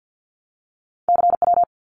Morse code, heard as a keyed tone, about 50 wpm, 700 Hz. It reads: CW
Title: Abbreviations Repeated 3x - 50wpm